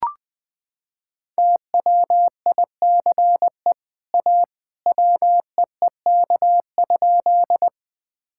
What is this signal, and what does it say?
Morse code, heard as a keyed tone, about 20 wpm, 700 Hz. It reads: TWICE A WEEK?
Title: Sentences from Top 1600 Words 20wpm